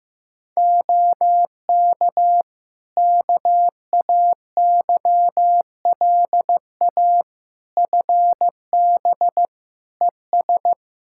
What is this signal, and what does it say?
Morse code, heard as a keyed tone, about 15 wpm, 700 Hz. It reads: OK KAYLA FB ES
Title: K1USN SST QSO Practice 15wpm